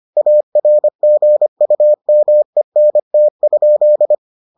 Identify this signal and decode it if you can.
Morse code, heard as a keyed tone, about 25 wpm, 600 Hz. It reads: ARGUMENT?